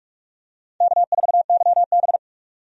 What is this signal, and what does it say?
Morse code, heard as a keyed tone, about 46 wpm, 700 Hz. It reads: K4YB
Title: USA Call Signs - 4 Characters - 45wpm